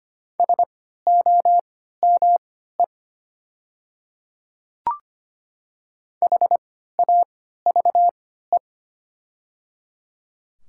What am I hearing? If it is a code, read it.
Morse code, heard as a keyed tone, about 25 wpm, spaced as for 14 wpm, 700 Hz. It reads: SOME HAVE
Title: 3x Character Spacing - Top 200 Words - 25wpm